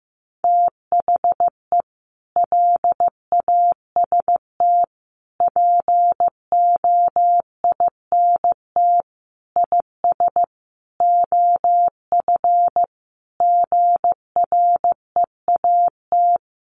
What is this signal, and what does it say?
Morse code, heard as a keyed tone, about 15 wpm, 700 Hz. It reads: THE LAST POINT IS OF GREAT
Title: Sentences from Top 1200 Words 15wpm